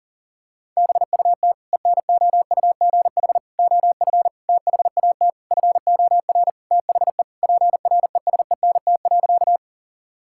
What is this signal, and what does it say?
Morse code, heard as a keyed tone, about 40 wpm, 700 Hz. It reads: BUT ENOUGH OF THAT FOR THE PRESENT.
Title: Sentences from Top 1000 Words - Review Part 5 - 40wpm